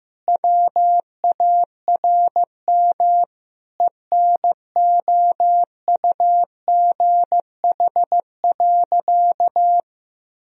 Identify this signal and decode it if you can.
Morse code, heard as a keyed tone, about 15 wpm, 700 Hz. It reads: WARM ENOUGH.
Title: Sentences from Top 1000 Words - Review Part 2 - 15wpm